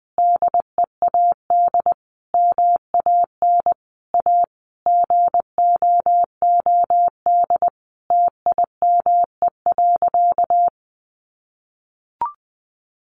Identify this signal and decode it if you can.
Morse code, heard as a keyed tone, about 20 wpm, 700 Hz. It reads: DEAD MAN A GOOD TIME.
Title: Sentences from Top 1500 Words 20wpm